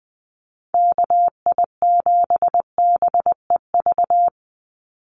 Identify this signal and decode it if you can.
Morse code, heard as a keyed tone, about 20 wpm, 700 Hz. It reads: KI7BEV
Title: USA Call Signs 20wpm